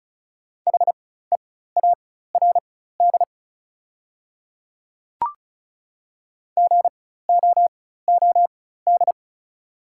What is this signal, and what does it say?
Morse code, heard as a keyed tone, about 34 wpm, spaced as for 16 wpm, 700 Hz. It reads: HEARD GOOD
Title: 4x Character Spacing - Top 500 Words - 35wpm